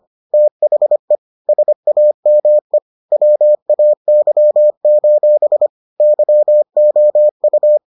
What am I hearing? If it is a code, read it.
Morse code, heard as a keyed tone, about 25 wpm, 600 Hz. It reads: THE SAME WAY: YOU